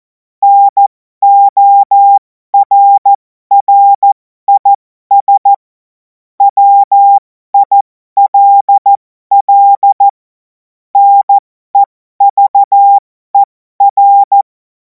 Morse code, heard as a keyed tone, about 14 wpm, spaced as for 12 wpm, 800 Hz. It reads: NORRIS WILL NEVER
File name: Morsekurs / Witze